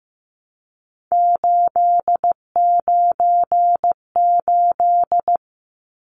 Morse code, heard as a keyed tone, about 15 wpm, 700 Hz. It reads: 898
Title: Numbers - 3-Digit 15wpm